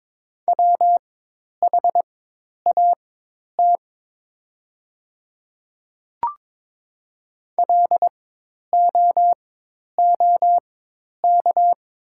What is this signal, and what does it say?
Morse code, heard as a keyed tone, about 22 wpm, spaced as for 10 wpm, 700 Hz. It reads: WHAT LOOK
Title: 4x Character Spacing - Top 200 Words - 22wpm